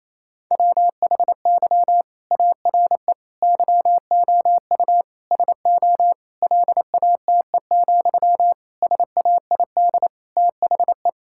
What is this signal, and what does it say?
Morse code, heard as a keyed tone, about 28 wpm, 700 Hz. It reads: WHY ARE YOU SO LATE, SAID THE
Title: Sentences from Top 1000 Words - Review Part 6 - 28wpm